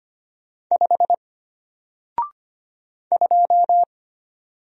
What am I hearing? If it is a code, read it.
Morse code, heard as a keyed tone, about 25 wpm, 700 Hz. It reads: 5 2
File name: Single-Letter-Number-25wpm